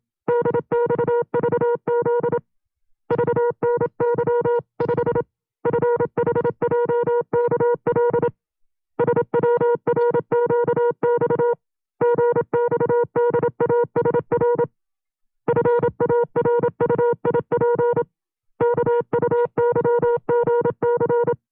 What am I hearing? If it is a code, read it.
Morse code, heard as a keyed tone, about 27 wpm, 480 Hz. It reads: DXVZ VNY5 FHJKL SWRQX GXDASR FARUIP KUYGC
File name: FreeFiles / CW-practice / 25WPM-side2